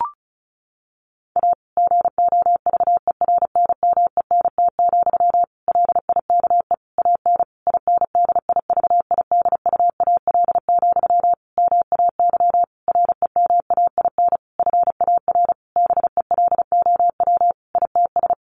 Morse code, heard as a keyed tone, about 34 wpm, 700 Hz. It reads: A GOVERNMENT, LIKE AN INDIVIDUAL, MAY REMAIN FAR BELOW ITS
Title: Sentences from Top 1400 Words 35wpm